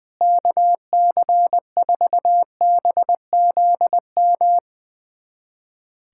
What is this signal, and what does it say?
Morse code, heard as a keyed tone, about 21 wpm, 700 Hz. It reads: KC4BZM
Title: USA Callsign DE Callsign - 6 Characters - 20wpm